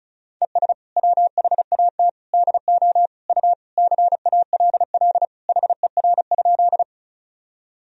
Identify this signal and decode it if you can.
Morse code, heard as a keyed tone, about 34 wpm, 700 Hz. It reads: ES WHAT DO U CALL HER?
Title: Sentences from Top 100 Words and Abbreviations 35wpm